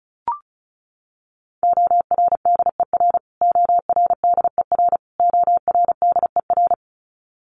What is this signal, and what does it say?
Morse code, heard as a keyed tone, about 34 wpm, 700 Hz. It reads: ORDER ORDER ORDER
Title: Top 500 Words Repeated 35wpm